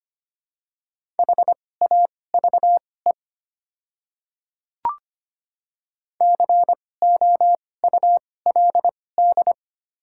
Morse code, heard as a keyed tone, about 25 wpm, spaced as for 18 wpm, 690 Hz. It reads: HAVE COULD
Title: 2x Character Spacing - Top 100 Words - 25wpm